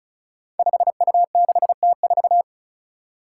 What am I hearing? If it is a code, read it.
Morse code, heard as a keyed tone, about 36 wpm, 700 Hz. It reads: 5U6T4
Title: Five Challenging Letter-Number Combinations 35wpm